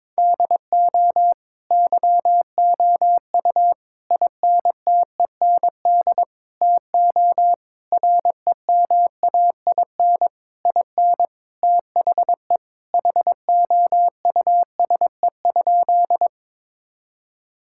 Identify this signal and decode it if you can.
Morse code, heard as a keyed tone, about 22 wpm, 700 Hz. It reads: DO YOU INTEND TO REMAIN IN THE HOUSE?
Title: Sentences from Top 1600 Words 22wpm